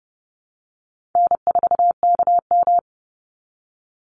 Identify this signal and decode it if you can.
Morse code, heard as a keyed tone, about 30 wpm, 700 Hz. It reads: N4KM
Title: Easy Call Signs Repeated 2x - 30wpm